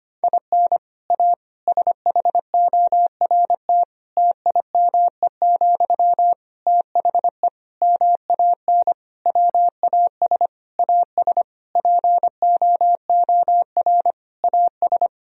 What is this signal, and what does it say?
Morse code, heard as a keyed tone, about 25 wpm, 700 Hz. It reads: IN A SHORT TIME, THE MAN WAS AS POOR AS
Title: Sentences from Top 1000 Words - Review Part 2 - 25wpm